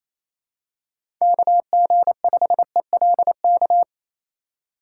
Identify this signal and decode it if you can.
Morse code, heard as a keyed tone, about 28 wpm, 700 Hz. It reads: KG5ELK
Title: USA Call Signs - 4x Character Spacing - 28wpm